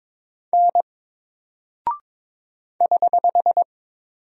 Morse code, heard as a keyed tone, about 22 wpm, 700 Hz. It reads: N <HH>
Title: Single Character (Letter:Number:Punctuation:Prosign) 22wpm